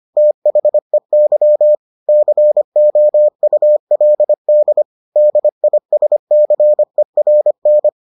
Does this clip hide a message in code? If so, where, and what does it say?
Yes: Morse code from 0.2 to 7.9 s, heard as a keyed tone, about 25 wpm, 600 Hz. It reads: THEY COULD DISCERN